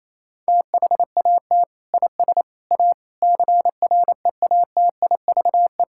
Morse code, heard as a keyed tone, about 28 wpm, 700 Hz. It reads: THAT IS A CREATIVE
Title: Sentences from Top 1000 Words - Review Part 5 - 28wpm